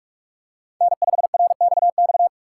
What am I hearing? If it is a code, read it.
Morse code, heard as a keyed tone, about 43 wpm, 700 Hz. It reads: N5RXX
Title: USA Call Signs - 5 Characters - 3x Character Spacing - 45wpm